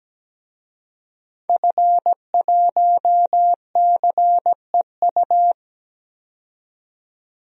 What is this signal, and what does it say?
Morse code, heard as a keyed tone, about 17 wpm, 700 Hz. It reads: F1CEU
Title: DXCC-100 Call Signs - 5 Characters - 17wpm